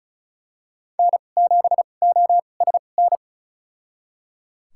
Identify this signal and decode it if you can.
Morse code, heard as a keyed tone, about 36 wpm, spaced as for 25 wpm, 700 Hz. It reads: N7OSN
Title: USA Call Signs - 5 Characters - 2x Character Spacing - 35wpm